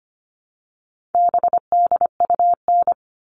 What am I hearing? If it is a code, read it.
Morse code, heard as a keyed tone, about 25 wpm, 700 Hz. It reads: BDUN
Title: All Cut Numbers - 4-Digit 25wpm